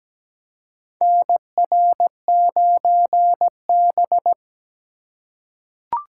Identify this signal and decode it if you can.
Morse code, heard as a keyed tone, about 17 wpm, 700 Hz. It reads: NR9B
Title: USA Call Signs - 4 Characters - 4x Character Spacing - 17wpm